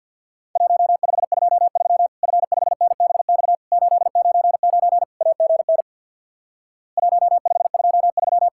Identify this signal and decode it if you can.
Morse code, heard as a keyed tone, about 50 wpm, 700 Hz. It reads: 1523 F5NBX 899 AGN 1523
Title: Worked All Europe DX Contest - QTC Exchange with Repeats - 50wpm